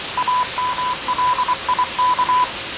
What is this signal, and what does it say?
Morse code, heard as a keyed tone, about 23 wpm, 1000 Hz. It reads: AMLIK
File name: 6824usb